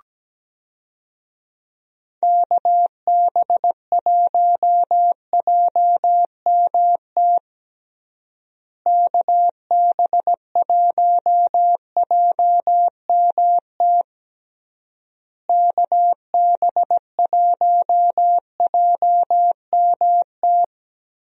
Morse code, heard as a keyed tone, about 17 wpm, 700 Hz. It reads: KB1JMT KB1JMT KB1JMT
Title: USA Call Signs - 6 Characters - Repeated 3x - 17wpm